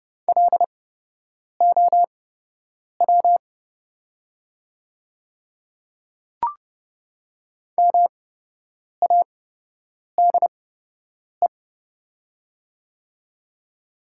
Morse code, heard as a keyed tone, about 30 wpm, spaced as for 8 wpm, 700 Hz. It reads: LOW MADE
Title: 8x Character Spacing - Top 300 Words - 30wpm